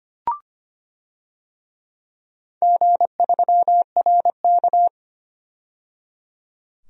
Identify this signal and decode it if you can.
Morse code, heard as a keyed tone, about 25 wpm, 700 Hz. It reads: G3RK
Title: DXCC-100 Call Signs - 4 Characters - 25wpm